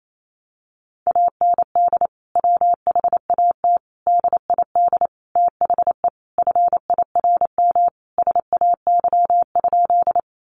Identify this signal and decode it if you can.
Morse code, heard as a keyed tone, about 28 wpm, 700 Hz. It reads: AND WHAT DID THE FIRM SAY?
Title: Sentences from Top 1400 Words 28wpm